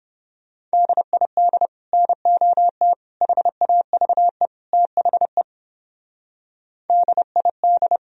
Morse code, heard as a keyed tone, about 31 wpm, 700 Hz. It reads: DID NOT HAVE THE DID